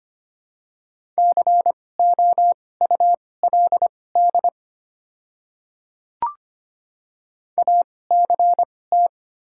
Morse code, heard as a keyed tone, about 25 wpm, spaced as for 18 wpm, 700 Hz. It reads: COULD ACT
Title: 2x Character Spacing - Top 300 Words - 25wpm